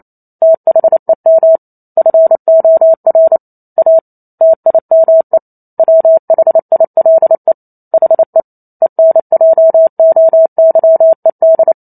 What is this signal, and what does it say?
Morse code, heard as a keyed tone, about 28 wpm, 650 Hz. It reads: THEM FOR A TIME WHILE HE ENJOYED